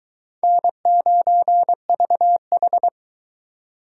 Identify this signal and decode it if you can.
Morse code, heard as a keyed tone, about 23 wpm, 700 Hz. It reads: N9VH